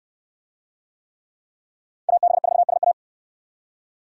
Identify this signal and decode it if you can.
Morse code, heard as a keyed tone, about 68 wpm, 700 Hz. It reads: AB4SA